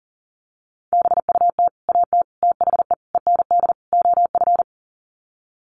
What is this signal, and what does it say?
Morse code, heard as a keyed tone, about 40 wpm, 700 Hz. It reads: BUT AT THE END OF
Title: Sets of 5 Words Encore2 40wpm